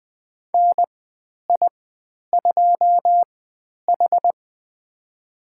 Morse code, heard as a keyed tone, about 20 wpm, spaced as for 10 wpm, 700 Hz. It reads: NI2H
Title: 095 Lesson 16g - Letter c - Call Signs 20wpm